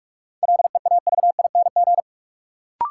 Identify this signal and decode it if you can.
Morse code, heard as a keyed tone, about 46 wpm, 700 Hz. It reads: LEAVING
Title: Sentences from Top 1000 Words - Review Part 4 - New Word Drill - 45wpm